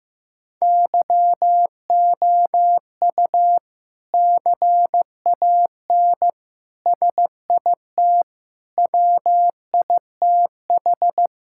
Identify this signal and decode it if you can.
Morse code, heard as a keyed tone, about 15 wpm, 700 Hz. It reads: YOU CAN SIT WITH
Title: Sentences from Top 1000 Words - Review Part 5 - 15wpm